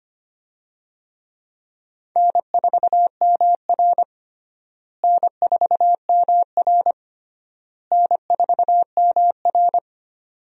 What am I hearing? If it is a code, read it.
Morse code, heard as a keyed tone, about 25 wpm, 700 Hz. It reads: N4MR N4MR N4MR
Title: USA Call Signs - 4 Characters - Repeated 3x - 25wpm